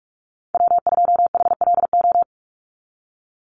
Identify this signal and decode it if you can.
Morse code, heard as a keyed tone, about 46 wpm, 690 Hz. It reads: W2HLO
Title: USA Call Signs 45wpm